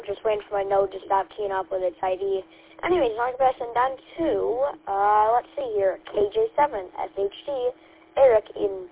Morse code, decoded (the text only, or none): none